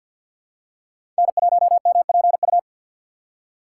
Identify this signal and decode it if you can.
Morse code, heard as a keyed tone, about 51 wpm, 700 Hz. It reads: N1MPU
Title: USA Call Signs - 5 Characters - 2x Character Spacing - 50wpm